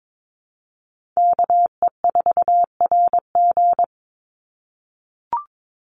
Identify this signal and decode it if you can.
Morse code, heard as a keyed tone, about 22 wpm, 700 Hz. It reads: KE4RG
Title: Call Signs - 22wpm